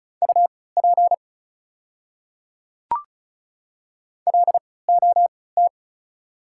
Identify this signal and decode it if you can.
Morse code, heard as a keyed tone, about 36 wpm, spaced as for 20 wpm, 700 Hz. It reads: UP LOT
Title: 3x Character Spacing - Top 500 Words - 35wpm